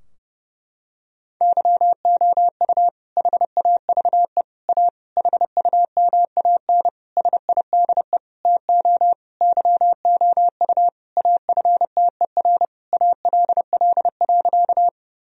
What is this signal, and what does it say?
Morse code, heard as a keyed tone, about 30 wpm, 700 Hz. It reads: YOU HAVE A HUMAN SIDE TO YOU AFTER ALL.
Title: Sentences from Top 1000 Words - Review Part 3 - 30wpm